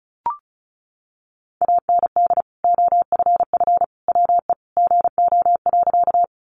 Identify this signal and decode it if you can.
Morse code, heard as a keyed tone, about 34 wpm, 700 Hz. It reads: AND OFF WE GO.
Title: Sentences from Top 200 Words 35wpm